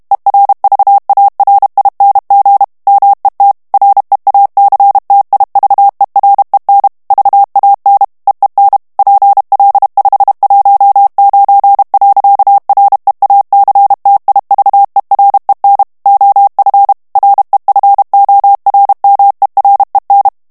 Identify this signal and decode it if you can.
Morse code, heard as a keyed tone, about 32 wpm, 800 Hz. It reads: ERVARING MET REACTIVEREN VAN EEN PL519. REACTIVEREN OF REFORMEREN
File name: morse mp3 / HOLLANDAIS texte clair 2 vitesse 30 mots minute